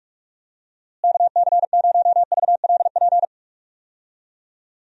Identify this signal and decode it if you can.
Morse code, heard as a keyed tone, about 46 wpm, 700 Hz. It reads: KC0VLP
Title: USA Call Signs - 6 Characters - Repeated 2x - 45wpm